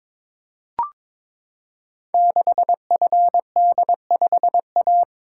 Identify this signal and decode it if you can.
Morse code, heard as a keyed tone, about 22 wpm, 700 Hz. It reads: 6FD5A
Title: Code Groups - 5 Letter:Numbers - 22wpm